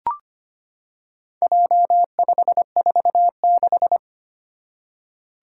Morse code, heard as a keyed tone, about 25 wpm, 700 Hz. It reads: J546
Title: Four Challenging Letter-Number Combinations 25wpm